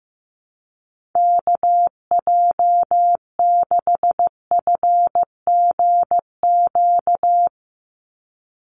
Morse code, heard as a keyed tone, about 15 wpm, 700 Hz. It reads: KJ6FGQ
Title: USA Call Signs 15wpm